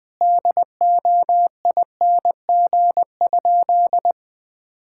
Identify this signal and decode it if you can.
Morse code, heard as a keyed tone, about 20 wpm, 700 Hz. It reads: DOING?
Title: Sentences with 3-Word Phrases - 20wpm